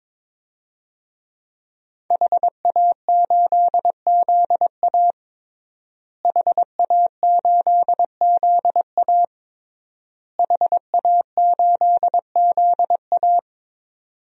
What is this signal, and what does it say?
Morse code, heard as a keyed tone, about 22 wpm, 700 Hz. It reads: HA8ZA HA8ZA HA8ZA
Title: DXCC-100 Call Signs - 5 Characters - Repeated 3x - 22wpm